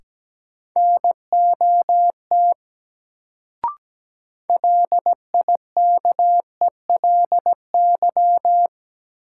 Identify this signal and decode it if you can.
Morse code, heard as a keyed tone, about 17 wpm, 700 Hz. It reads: NOT LIKELY
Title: Sets of 2 Words - New Word Drill - 17wpm